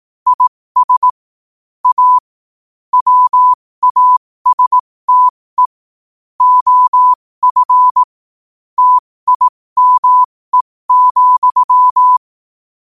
Morse code, heard as a keyed tone, about 18 wpm, spaced as for 15 wpm, 1000 Hz. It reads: IS A WASTE OF TIME,
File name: qotdfiles / 15wpm